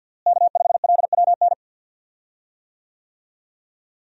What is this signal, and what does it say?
Morse code, heard as a keyed tone, about 52 wpm, 700 Hz. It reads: K5LWN
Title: USA Call Signs - 5 Characters - Repeated 3x - 50wpm